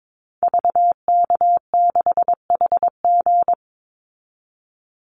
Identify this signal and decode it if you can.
Morse code, heard as a keyed tone, about 22 wpm, 700 Hz. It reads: VK6HG
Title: Call Signs Repeated 2x - 22wpm